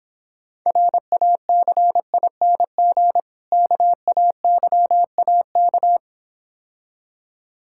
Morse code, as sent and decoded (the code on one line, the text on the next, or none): .-. .- -.-. .. -. --. / -.- .- -.-- .- -.-
RACING KAYAK